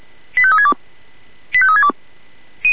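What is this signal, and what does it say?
Five-tone selective call (ZVEI), heard as quick runs of five tones, tones of about 70 ms. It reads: 85362 85362
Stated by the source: ZVEI five-tone call received on SDR.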